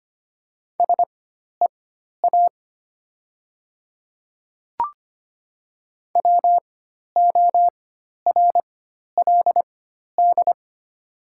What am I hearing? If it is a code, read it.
Morse code, heard as a keyed tone, about 25 wpm, spaced as for 12 wpm, 700 Hz. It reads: SEA WORLD
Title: 4x Character Spacing - Top 300 Words - 25wpm